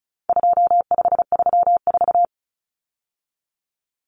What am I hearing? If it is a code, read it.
Morse code, heard as a keyed tone, about 34 wpm, 700 Hz. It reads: 2534
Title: Numbers - 4-Digit 35wpm